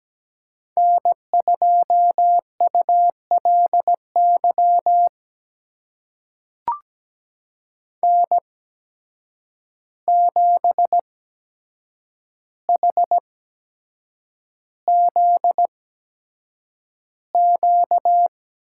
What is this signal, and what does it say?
Morse code, heard as a keyed tone, about 17 wpm, spaced as for 7 wpm, 700 Hz. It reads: N2ULY N7HZQ
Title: USA Call Signs - 5 Characters - 8x Character Spacing - 17wpm